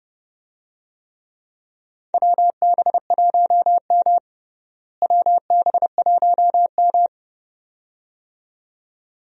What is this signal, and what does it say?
Morse code, heard as a keyed tone, about 30 wpm, 700 Hz. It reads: WB1M WB1M
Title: USA Call Signs - 4 Characters - Repeated 2x - 30wpm